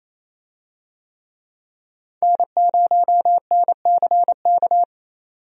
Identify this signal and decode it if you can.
Morse code, heard as a keyed tone, about 28 wpm, 700 Hz. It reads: N0NCK
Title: USA Call Signs - 5 Characters - 28wpm